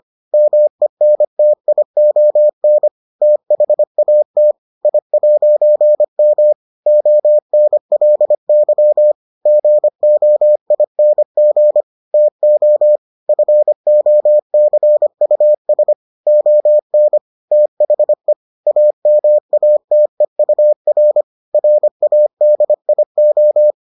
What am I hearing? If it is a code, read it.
Morse code, heard as a keyed tone, about 25 wpm, 600 Hz. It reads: MENTION THAT I'M ONLY GOING TO FOCUS ON THE AMATEUR RADIO